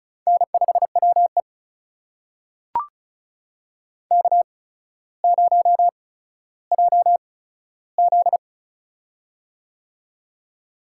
Morse code, heard as a keyed tone, about 34 wpm, spaced as for 15 wpm, 700 Hz. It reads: N5WE K0JZ
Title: USA Call Signs - 4 Characters - 8x Character Spacing - 35wpm